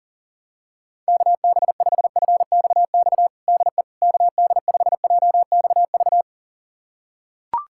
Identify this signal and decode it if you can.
Morse code, heard as a keyed tone, about 40 wpm, 700 Hz. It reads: KB5FXX DE KD5JXV
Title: USA Callsign DE Callsign - 6 Characters - 40wpm